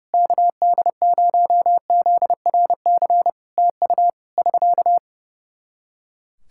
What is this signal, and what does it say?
Morse code, heard as a keyed tone, about 28 wpm, 700 Hz. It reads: KD0ZRC TU <SK>
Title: K1USN SST QSO Practice 30wpm